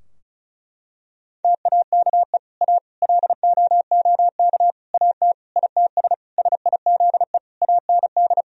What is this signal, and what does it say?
Morse code, heard as a keyed tone, about 36 wpm, 700 Hz. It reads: TAKE A LOOK AT ITS SIZE AND
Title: Sentences from Top 1000 Words - Review Part 2 - 35wpm